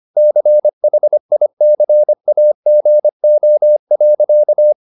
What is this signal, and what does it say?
Morse code, heard as a keyed tone, about 25 wpm, 600 Hz. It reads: CHICAGO.
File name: the old man and the sea part 1 25wpm